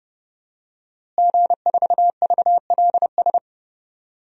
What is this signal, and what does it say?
Morse code, heard as a keyed tone, about 31 wpm, 700 Hz. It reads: G4VLS
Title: DXCC-100 Call Signs - 5 Characters - Repeated 2x - 30wpm